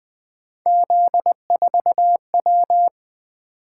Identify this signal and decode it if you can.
Morse code, heard as a keyed tone, about 20 wpm, 700 Hz. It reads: Z4W
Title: Three Letter-Number Combinations - 20wpm